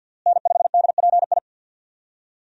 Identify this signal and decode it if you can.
Morse code, heard as a keyed tone, about 50 wpm, 700 Hz. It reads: N5DPI
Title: USA Call Signs - 5 Characters - Repeated 3x - 50wpm